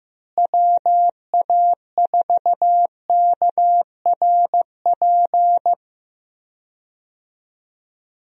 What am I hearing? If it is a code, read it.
Morse code, heard as a keyed tone, about 15 wpm, 700 Hz. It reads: WA4KRP
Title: USA Call Signs - 6 Characters - Repeated 3x - 15wpm